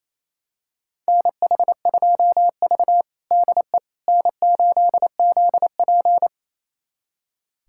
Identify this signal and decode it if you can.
Morse code, heard as a keyed tone, about 28 wpm, 700 Hz. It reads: NH2V DE N8ZP
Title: USA Callsign DE Callsign - 4 Characters - 28wpm